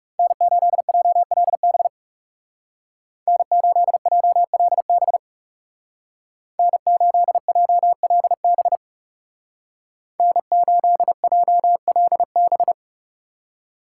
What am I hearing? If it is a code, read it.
Morse code, heard as a keyed tone, about 36 wpm, 700 Hz. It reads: N8JLB N8JLB N8JLB N8JLB
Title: Speed Racing - USA 1x3 Call Signs 30wpm